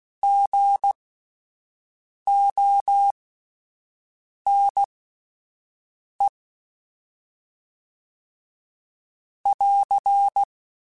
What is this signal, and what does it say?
Morse code, heard as a keyed tone, about 16 wpm, spaced as for 5 wpm, 800 Hz. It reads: GONE <AR>
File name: Misc / Misc-mp3s / Gotten @10